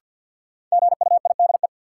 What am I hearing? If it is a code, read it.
Morse code, heard as a keyed tone, about 52 wpm, 700 Hz. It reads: GUIDE
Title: Sentences from Top 600 Words - New Word Drill - 50wpm